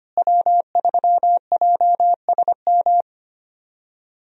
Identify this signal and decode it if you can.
Morse code, heard as a keyed tone, about 25 wpm, 700 Hz. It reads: W3JSM
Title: USA Call Signs - 5 Characters - Repeated 3x - 25wpm